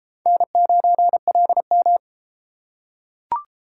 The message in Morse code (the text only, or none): N9LM